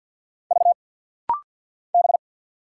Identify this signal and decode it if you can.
Morse code, heard as a keyed tone, about 48 wpm, 700 Hz. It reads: V B
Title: Single Letter-Number - Rapid-Fire 50wpm